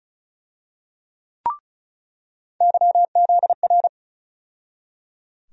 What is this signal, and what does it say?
Morse code, heard as a keyed tone, about 36 wpm, 700 Hz. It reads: YZR
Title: USA Call Sign Postfixes - 3 Letters 35wpm